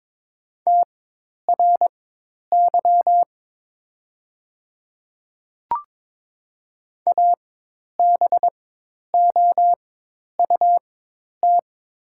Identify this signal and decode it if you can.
Morse code, heard as a keyed tone, about 22 wpm, spaced as for 10 wpm, 700 Hz. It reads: TRY ABOUT
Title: 4x Character Spacing - Top 200 Words - 22wpm